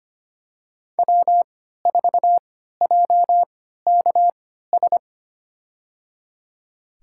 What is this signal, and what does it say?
Morse code, heard as a keyed tone, about 25 wpm, spaced as for 14 wpm, 700 Hz. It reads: W4JKS
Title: USA Call Signs - 3x Character Spacing - 25wpm